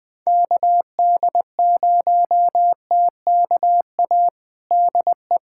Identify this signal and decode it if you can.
Morse code, heard as a keyed tone, about 20 wpm, 700 Hz. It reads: KD0TKA DE
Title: USA Callsign DE Callsign - 6 Characters - 20wpm